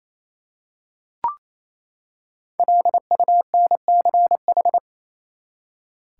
Morse code, heard as a keyed tone, about 28 wpm, 700 Hz. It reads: LUNCH
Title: Five Letter Words - 28wpm